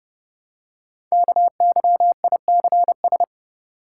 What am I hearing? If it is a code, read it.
Morse code, heard as a keyed tone, about 30 wpm, 700 Hz. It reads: KYICS
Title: Code Groups - 5 Letters - 30wpm